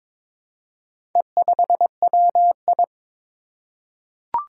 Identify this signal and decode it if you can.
Morse code, heard as a keyed tone, about 22 wpm, 700 Hz. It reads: E5WI
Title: Four Challenging Letter-Number Combinations 22wpm